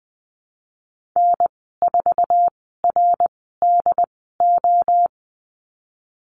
Morse code, heard as a keyed tone, about 20 wpm, spaced as for 15 wpm, 700 Hz. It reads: N4RDO
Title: 076 Lesson 13g - Letter L - Call Signs 20wpm